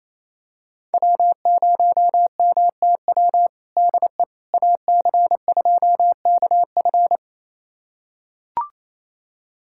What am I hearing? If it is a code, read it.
Morse code, heard as a keyed tone, about 28 wpm, 700 Hz. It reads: W0MTW DE AC2KF
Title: USA Callsign DE Callsign - 5 Characters - 28wpm